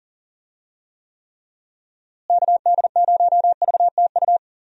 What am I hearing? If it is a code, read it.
Morse code, heard as a keyed tone, about 39 wpm, 700 Hz. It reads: KD0VTU
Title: USA Call Signs - 6 Characters - Repeated 3x - 40wpm